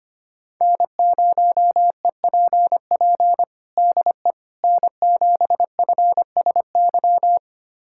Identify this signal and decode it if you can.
Morse code, heard as a keyed tone, about 25 wpm, 700 Hz. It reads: N0EPP DE N7FSY
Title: USA Callsign DE Callsign - 5 Characters - 25wpm